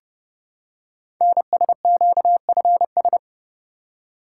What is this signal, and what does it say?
Morse code, heard as a keyed tone, about 29 wpm, 700 Hz. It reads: NSQFS
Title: Code Groups - 5 Letters - 30wpm